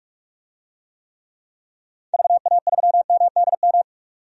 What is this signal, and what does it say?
Morse code, heard as a keyed tone, about 46 wpm, 690 Hz. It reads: VA3MDM
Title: DXCC-100 Call Signs - 6 Characters - 45wpm